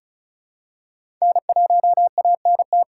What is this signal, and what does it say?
Morse code, heard as a keyed tone, about 36 wpm, 700 Hz. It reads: N1ANT
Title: USA Call Signs - 5 Characters - 4x Character Spacing - 35wpm